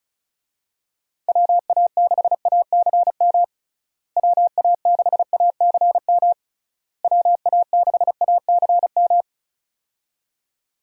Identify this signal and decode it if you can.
Morse code, heard as a keyed tone, about 34 wpm, 700 Hz. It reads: WA6ACM WA6ACM WA6ACM
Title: USA Call Signs - 6 Characters - Repeated 3x - 35wpm